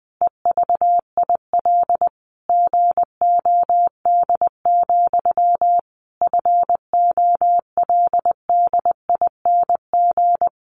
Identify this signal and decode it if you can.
Morse code, heard as a keyed tone, about 19 wpm, 700 Hz. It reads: EVIL GOD, FOLDING
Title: A Sound Of Thunder 20wpm